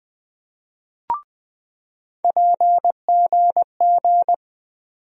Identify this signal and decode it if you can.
Morse code, heard as a keyed tone, about 20 wpm, 700 Hz. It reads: PGG
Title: USA Call Sign Postfixes - 3 Letters 20wpm